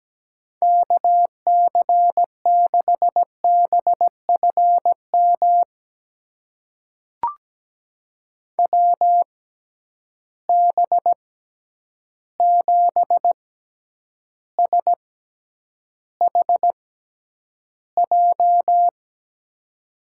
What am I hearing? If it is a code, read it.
Morse code, heard as a keyed tone, about 17 wpm, spaced as for 9 wpm, 700 Hz. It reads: KC6BFM WB7SHJ
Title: USA Call Signs - 6 Characters - 6x Character Spacing - 17wpm